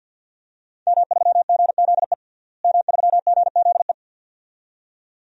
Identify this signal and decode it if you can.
Morse code, heard as a keyed tone, about 50 wpm, 700 Hz. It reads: M3GZE M3GZE
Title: DXCC-100 Call Signs - 5 Characters - Repeated 2x - 50wpm